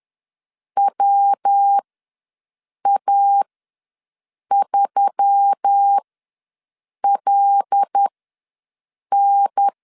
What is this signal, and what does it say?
Morse code, heard as a keyed tone, about 10 wpm, spaced as for 6 wpm, 800 Hz. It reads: WA3LN
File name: LEZIONI / 5 LBVC / 5 L ws 10wpm 3636